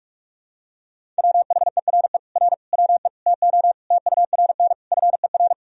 Morse code, heard as a keyed tone, about 41 wpm, 700 Hz. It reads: WHERE R WE TO TURN FER
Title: Sentences from Top 200 Words and Abbreviations 45wpm